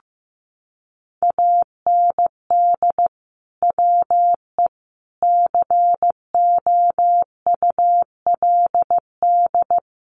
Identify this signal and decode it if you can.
Morse code, heard as a keyed tone, about 15 wpm, 700 Hz. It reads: AND WE COULD
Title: Sentences from Top 1200 Words 15wpm